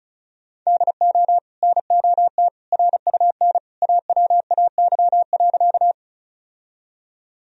Morse code, heard as a keyed tone, about 34 wpm, 700 Hz. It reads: DO NOT RUN AWAY.
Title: Sentences from Top 1000 Words - Review Part 1 - 35wpm